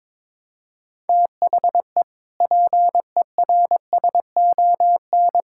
Morse code, heard as a keyed tone, about 21 wpm, 700 Hz. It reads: THE PERSON
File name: practice / 22wpm / Sentences with 8-Word Phrases - 22wpm